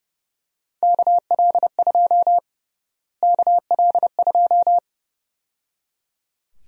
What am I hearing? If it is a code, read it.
Morse code, heard as a keyed tone, about 31 wpm, 700 Hz. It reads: KL2 KL2
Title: USA Call Sign Prefixes - Repeated 2x - 30wpm